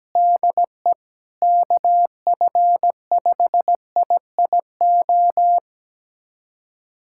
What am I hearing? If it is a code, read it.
Morse code, heard as a keyed tone, about 17 wpm, 700 Hz. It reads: DE KF5IIO
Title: USA Callsign DE Callsign - 6 Characters - 17wpm